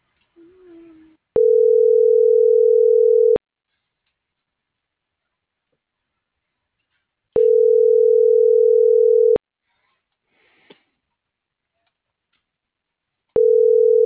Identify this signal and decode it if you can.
Ringback tone (North America), 440 + 480 Hz.